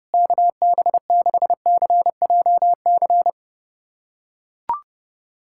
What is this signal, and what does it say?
Morse code, heard as a keyed tone, about 30 wpm, 700 Hz. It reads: KB6CJC
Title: USA Callsign DE Callsign - 6 Characters - 30wpm